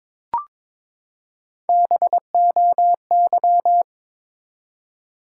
Morse code, heard as a keyed tone, about 22 wpm, 700 Hz. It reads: BOY
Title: Three Letter Words - 22wpm